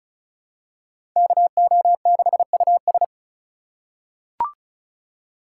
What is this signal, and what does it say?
Morse code, heard as a keyed tone, about 34 wpm, 700 Hz. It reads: KO6US
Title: USA Call Signs - 5 Characters - 3x Character Spacing - 35wpm